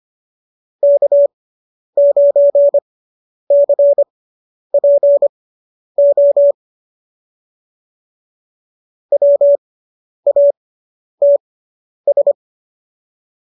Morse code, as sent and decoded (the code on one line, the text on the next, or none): -.- ----. -.-. .--. --- / .-- .- - ...
K9CPO WATS